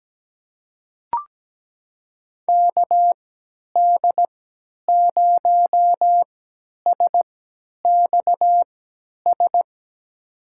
Morse code, heard as a keyed tone, about 17 wpm, spaced as for 10 wpm, 700 Hz. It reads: KD0SXS
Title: USA Call Signs - 6 Characters - 3x Character Spacing - 17wpm